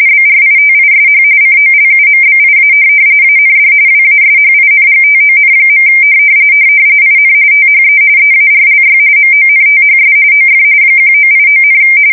RTTY, about 45.45 baud. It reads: BRITISH COLUMBIA AND WASHINGTON. IN THE EARLY MONTHS OF 1989,
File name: fsk170 / USS-Missouri